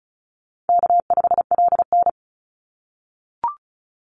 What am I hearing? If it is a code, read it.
Morse code, heard as a keyed tone, about 34 wpm, 700 Hz. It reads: K5LN
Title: Call Signs - 35wpm